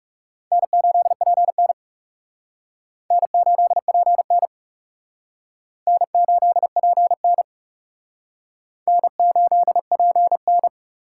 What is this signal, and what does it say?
Morse code, heard as a keyed tone, about 40 wpm, 700 Hz. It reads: N8PN N8PN N8PN N8PN